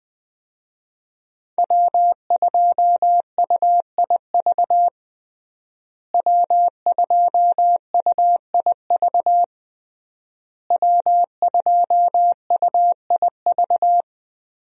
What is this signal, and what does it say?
Morse code, heard as a keyed tone, about 21 wpm, 700 Hz. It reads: W2UIV W2UIV W2UIV
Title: USA Call Signs - 5 Characters - Repeated 3x - 20wpm